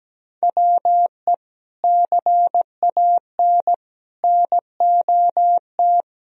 Morse code, heard as a keyed tone, about 17 wpm, 700 Hz. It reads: WE CAN NOT
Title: Sentences from Top 1000 Words - Review Part 2 - 17wpm